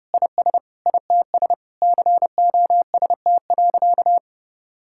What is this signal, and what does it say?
Morse code, heard as a keyed tone, about 30 wpm, 700 Hz. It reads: IS ITS COST.
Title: Sentences from Top 1000 Words - Review Part 6 - 30wpm